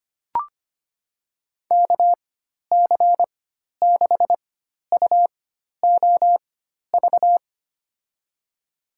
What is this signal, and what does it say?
Morse code, heard as a keyed tone, about 25 wpm, spaced as for 12 wpm, 700 Hz. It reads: KC6UOV
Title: USA Call Signs - 6 Characters - 4x Character Spacing - 25wpm